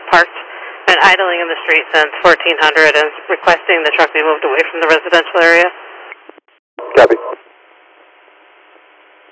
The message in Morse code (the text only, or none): none